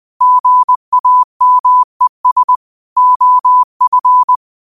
Morse code, heard as a keyed tone, about 20 wpm, 1000 Hz. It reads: GAMES OF
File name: qotdfiles / 20wpm